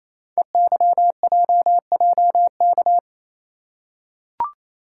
Morse code, heard as a keyed tone, about 28 wpm, 700 Hz. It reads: EYJJK
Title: Code Groups - 5 Letters - 28wpm